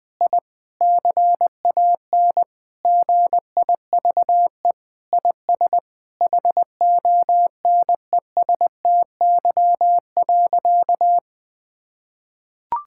Morse code, heard as a keyed tone, about 20 wpm, 700 Hz. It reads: I CAN GIVE IS HONESTY.
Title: Sentences with 8-Word Phrases - Spoken Beginning - 20wpm